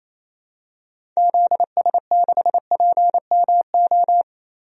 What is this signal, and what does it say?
Morse code, heard as a keyed tone, about 28 wpm, 700 Hz. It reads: ZS6PMO
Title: DXCC-100 Call Signs - 6 Characters - Repeated 2x - 28wpm